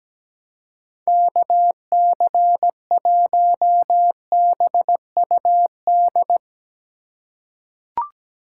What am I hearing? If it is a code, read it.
Morse code, heard as a keyed tone, about 17 wpm, 700 Hz. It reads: KC1BUD
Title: USA Call Signs - 4x Character Spacing - 17wpm